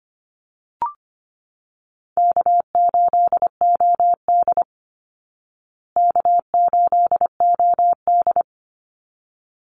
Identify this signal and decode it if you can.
Morse code, heard as a keyed tone, about 25 wpm, 700 Hz. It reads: K8OD K8OD
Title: Call Signs Repeated 3x - 25wpm